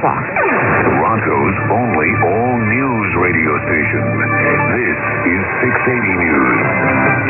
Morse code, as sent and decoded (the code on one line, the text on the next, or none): none